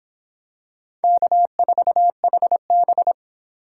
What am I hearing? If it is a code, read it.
Morse code, heard as a keyed tone, about 26 wpm, 700 Hz. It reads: K4HB